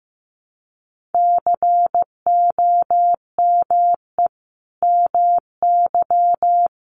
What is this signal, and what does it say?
Morse code, heard as a keyed tone, about 15 wpm, 700 Hz. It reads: COME MY
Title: Sentences from Top 1300 Words 15wpm